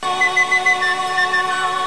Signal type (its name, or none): none